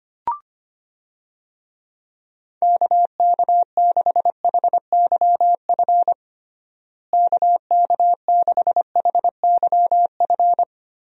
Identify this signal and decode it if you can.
Morse code, heard as a keyed tone, about 25 wpm, 700 Hz. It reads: KK6HYF KK6HYF
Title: USA Call Signs - 6 Characters - Repeated 3x - 25wpm